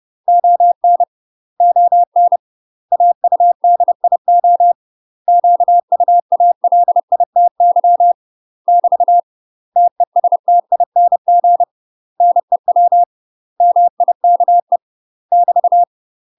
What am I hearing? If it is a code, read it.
Morse code, heard as a keyed tone, about 30 wpm, 700 Hz. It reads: ON ON AUDIO QUALITY <BT> TESTING NEW MIKE <BT>
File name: UK MASTER QSOs TRAINING / 30WPM / UK MASTER QSOs 0041